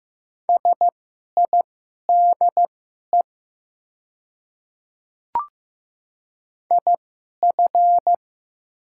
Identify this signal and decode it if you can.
Morse code, heard as a keyed tone, about 15 wpm, spaced as for 11 wpm, 700 Hz. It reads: SIDE IF
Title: 2x Character Spacing - Top 100 Words - 15wpm